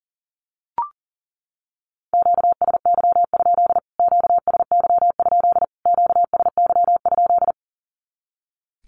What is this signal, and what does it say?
Morse code, heard as a keyed tone, about 40 wpm, 700 Hz. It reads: QSY? QSY? QSY?
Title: Q-Codes Repeated 3x 40wpm